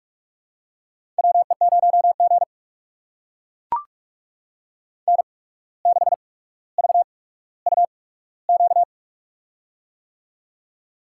Morse code, heard as a keyed tone, about 44 wpm, spaced as for 18 wpm, 700 Hz. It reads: WE0G N6VUQ
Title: USA Call Signs - 8x Character Spacing - 45wpm